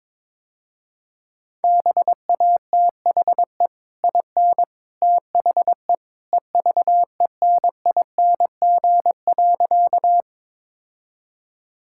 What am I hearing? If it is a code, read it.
Morse code, heard as a keyed tone, about 21 wpm, 700 Hz. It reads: BATHE IN THE EVENING.
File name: practice / 22wpm / Easy Sentences with Hints - 22wpm